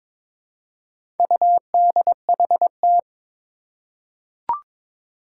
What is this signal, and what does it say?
Morse code, heard as a keyed tone, about 23 wpm, 700 Hz. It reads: UDHT
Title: Four Challenging Letter-Number Combinations 22wpm